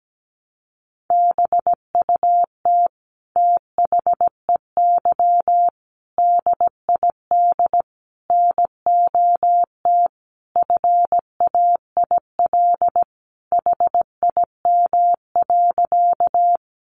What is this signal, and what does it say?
Morse code, heard as a keyed tone, about 17 wpm, 700 Hz. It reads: BUT THEY DID NOT FAIL HIM.
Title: Sentences from Top 1100 Words 17wpm